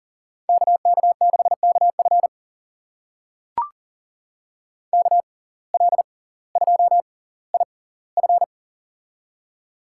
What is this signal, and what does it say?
Morse code, heard as a keyed tone, about 40 wpm, spaced as for 20 wpm, 700 Hz. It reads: KK6KF KL2IF
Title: USA Call Signs - 5 Characters - 6x Character Spacing - 40wpm